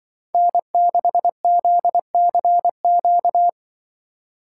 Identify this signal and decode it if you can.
Morse code, heard as a keyed tone, about 24 wpm, 700 Hz. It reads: N6ZCQ
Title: Speed Racing - USA Call Signs 20wpm